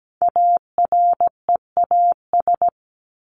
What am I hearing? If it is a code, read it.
Morse code, heard as a keyed tone, about 17 wpm, 700 Hz. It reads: AREAS
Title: ES:S Suffix - Top 100 Words with Base Words 17wpm